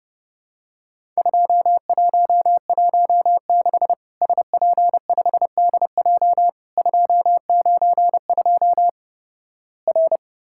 Morse code, heard as a keyed tone, about 32 wpm, 700 Hz. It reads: 2116 SP5DJ 292 R
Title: Worked All Europe DX Contest - QTC Exchange - 30wpm